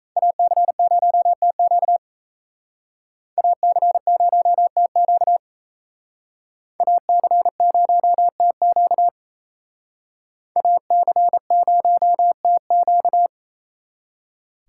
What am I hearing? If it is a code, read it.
Morse code, heard as a keyed tone, about 35 wpm, 700 Hz. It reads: AC0TQ AC0TQ AC0TQ AC0TQ
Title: Speed Racing - USA 2x2 Call Signs 28wpm